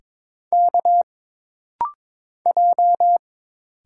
Morse code, heard as a keyed tone, about 22 wpm, 700 Hz. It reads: K J
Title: Single Letters - Rapid-Fire 22wpm